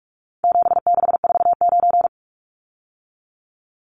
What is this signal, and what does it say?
Morse code, heard as a keyed tone, about 44 wpm, 700 Hz. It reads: 7649
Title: Numbers - 4-Digit 45wpm